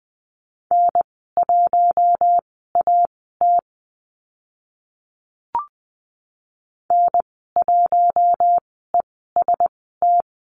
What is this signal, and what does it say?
Morse code, heard as a keyed tone, about 20 wpm, spaced as for 15 wpm, 700 Hz. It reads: N1AT N1EST
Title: 042 Lesson 8g - Number 1 - Call Signs 20wpm